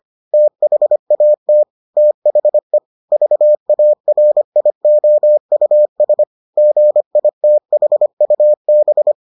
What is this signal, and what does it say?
Morse code, heard as a keyed tone, about 25 wpm, 600 Hz. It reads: THAT THE VARIOUS GITHUB